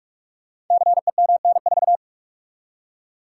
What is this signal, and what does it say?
Morse code, heard as a keyed tone, about 44 wpm, 700 Hz. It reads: CEMN4
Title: Code Groups - 5 Letter:Numbers - 45wpm